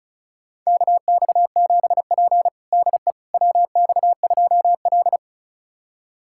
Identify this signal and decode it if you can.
Morse code, heard as a keyed tone, about 36 wpm, 700 Hz. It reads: KX7P DE WX2L
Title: USA Callsign DE Callsign - 4 Characters - 35wpm